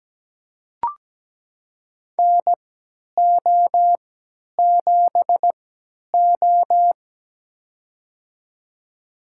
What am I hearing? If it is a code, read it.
Morse code, heard as a keyed tone, about 17 wpm, spaced as for 10 wpm, 700 Hz. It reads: NO7O
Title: USA Call Signs - 4 Characters - 3x Character Spacing - 17wpm